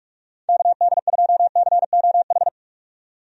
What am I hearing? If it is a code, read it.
Morse code, heard as a keyed tone, about 41 wpm, 700 Hz. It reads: KD2COH
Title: USA Call Signs - 6 Characters - 45wpm